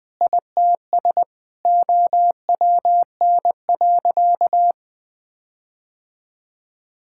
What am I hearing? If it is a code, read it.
Morse code, heard as a keyed tone, about 21 wpm, 700 Hz. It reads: ITS OWN.
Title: Sentences with 8-Word Phrases - Spoken Beginning - 20wpm